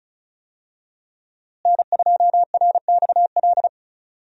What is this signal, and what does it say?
Morse code, heard as a keyed tone, about 36 wpm, 700 Hz. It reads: N2RXL
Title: USA Call Signs - 5 Characters - Repeated 2x - 35wpm